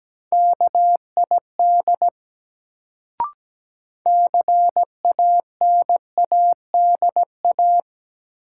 Morse code, heard as a keyed tone, about 17 wpm, 700 Hz. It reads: KID CANADA
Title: Sets of 5 Words - New Word Drill - 17wpm